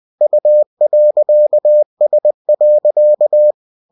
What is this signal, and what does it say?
Morse code, heard as a keyed tone, about 20 wpm, 600 Hz. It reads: U.S.
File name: cw-books / winnie-the-pooh / winnie-20wpm-page-0112